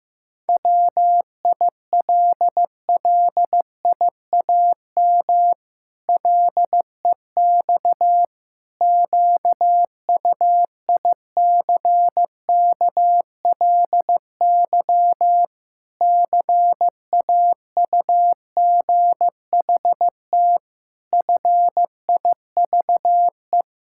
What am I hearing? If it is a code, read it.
Morse code, heard as a keyed tone, about 15 wpm, 700 Hz. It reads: WILLIAM LEX QUICKLY CAUGHT FIVE
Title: Pangram Challenge v2 - 15wpm